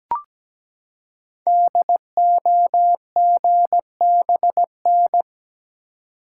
Morse code, heard as a keyed tone, about 17 wpm, 700 Hz. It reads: DOGBN
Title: Code Groups - 5 Letters - 17wpm